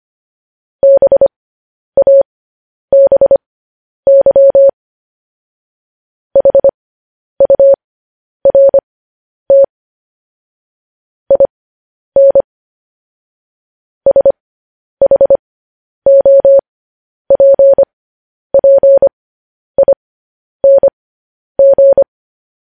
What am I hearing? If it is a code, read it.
Morse code, heard as a keyed tone, about 25 wpm, spaced as for 10 wpm, 550 Hz. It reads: BABY HURT IN SHOPPING
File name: podcast / News headlines-1x-25x10